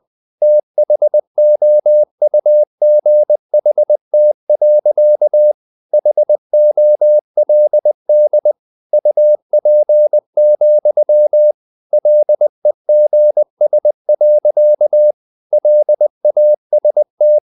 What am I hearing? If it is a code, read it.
Morse code, heard as a keyed tone, about 20 wpm, 600 Hz. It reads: THOUGHT. HOLD UP, LEGS. LAST
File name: the old man and the sea part 2 20wpm